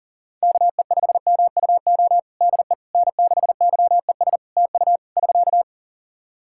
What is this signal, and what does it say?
Morse code, heard as a keyed tone, about 40 wpm, 700 Hz. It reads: KE5MUO DE N6YES TU <SK>
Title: K1USN SST QSO Practice 40wpm